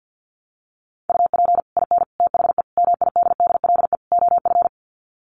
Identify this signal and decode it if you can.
Morse code, heard as a keyed tone, about 48 wpm, 700 Hz. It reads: UP IN THE MIDDLE OF
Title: Sets of 5 Words Encore 50wpm